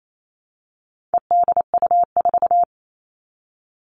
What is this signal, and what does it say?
Morse code, heard as a keyed tone, about 28 wpm, 700 Hz. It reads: EDU4
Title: All Cut Numbers - 4-Digit 28wpm